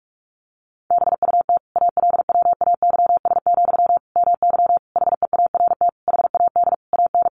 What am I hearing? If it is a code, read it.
Morse code, heard as a keyed tone, about 44 wpm, 700 Hz. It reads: BUT ALWAYS, MY HEART HAD AN
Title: Sentences from Top 1200 Words 45wpm